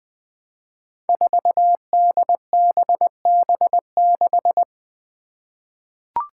Morse code, heard as a keyed tone, about 20 wpm, 700 Hz. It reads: 4DBB6
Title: Five Challenging Letter-Number Combinations 20wpm